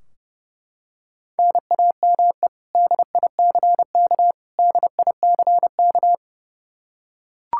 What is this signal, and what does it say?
Morse code, heard as a keyed tone, about 30 wpm, 700 Hz. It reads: NAME DICK DICK
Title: Practice QSO - Exchange 1 - Beginning - 30wpm